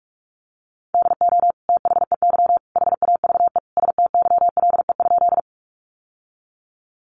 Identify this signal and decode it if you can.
Morse code, heard as a keyed tone, about 44 wpm, 700 Hz. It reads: DO THEY HAVE STYLE?
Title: Sentences from Top 800 Words 45wpm